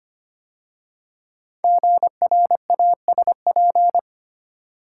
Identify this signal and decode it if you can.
Morse code, heard as a keyed tone, about 25 wpm, 700 Hz. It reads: GRASP
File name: 5 Letter Words - First 3 Letters Given - 25wpm